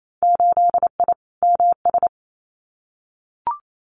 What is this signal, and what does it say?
Morse code, heard as a keyed tone, about 28 wpm, 700 Hz. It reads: 8I MS
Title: Winter Field Day Exchanges Encore 28wpm